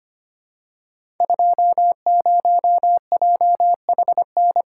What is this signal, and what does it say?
Morse code, heard as a keyed tone, about 25 wpm, 700 Hz. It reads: 20JHN
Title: Code Groups - 5 Letter:Numbers - 25wpm